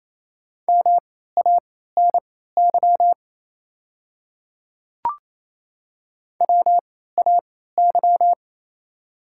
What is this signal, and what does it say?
Morse code, heard as a keyed tone, about 28 wpm, spaced as for 16 wpm, 700 Hz. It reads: MANY WAY
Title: 3x Character Spacing - Top 100 Words - 28wpm